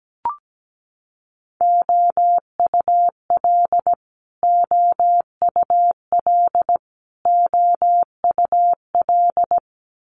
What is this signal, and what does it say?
Morse code, heard as a keyed tone, about 17 wpm, 700 Hz. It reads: OUL OUL OUL
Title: Multi-letter Phonemes Repeated 3x 17wpm